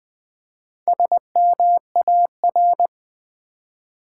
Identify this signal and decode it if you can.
Morse code, heard as a keyed tone, about 20 wpm, 700 Hz. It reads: SMAR
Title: Build A Word - 1 Letter At A Time - 5-Letter Words - 20wpm